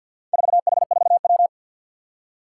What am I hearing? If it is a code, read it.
Morse code, heard as a keyed tone, about 53 wpm, 700 Hz. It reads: 4H4W
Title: Four Challenging Letter-Number Combinations 50wpm